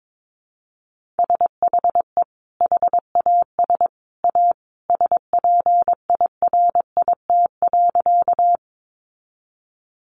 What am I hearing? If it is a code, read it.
Morse code, heard as a keyed tone, about 22 wpm, 700 Hz. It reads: SHE HAS A SPIRIT.
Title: Sentences from Top 1400 Words 22wpm